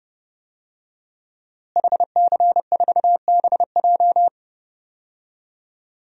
Morse code, heard as a keyed tone, about 30 wpm, 700 Hz. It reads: HC4BJ
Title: DXCC-100 Call Signs - 5 Characters - Repeated 2x - 30wpm